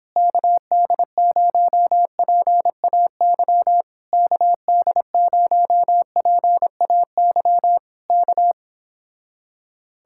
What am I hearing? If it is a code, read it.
Morse code, heard as a keyed tone, about 26 wpm, 700 Hz. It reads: KD0PAY KD0PAY K